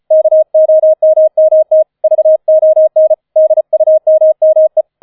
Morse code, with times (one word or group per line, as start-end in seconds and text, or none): 0.1-1.8 KOMMT
2.0-3.1 VON
3.4-4.8 DUMME